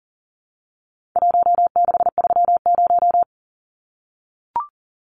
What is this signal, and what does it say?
Morse code, heard as a keyed tone, about 40 wpm, 700 Hz. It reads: 1630
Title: Numbers - Contest Style - 4-Digit 40wpm